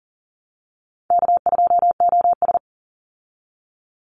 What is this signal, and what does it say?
Morse code, heard as a keyed tone, about 40 wpm, 700 Hz. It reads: K2OS
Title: Call Signs Repeated 3x - 40wpm